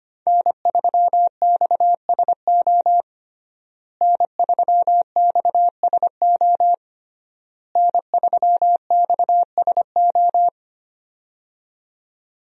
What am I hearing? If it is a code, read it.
Morse code, heard as a keyed tone, about 25 wpm, 700 Hz. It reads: N3XSO N3XSO N3XSO
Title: USA Call Signs - 5 Characters - Repeated 3x - 25wpm